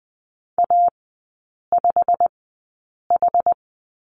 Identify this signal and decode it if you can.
Morse code, heard as a keyed tone, about 20 wpm, spaced as for 8 wpm, 700 Hz. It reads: A5H
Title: 087 Lesson 15e - Number 5 - Three Characters 20wpm